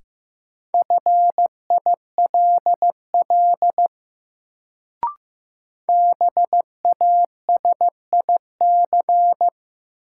Morse code, heard as a keyed tone, about 15 wpm, 700 Hz. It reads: FILL BASIC
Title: Sentences from Top 1000 Words - Review Part 3 - New Word Drill - 15wpm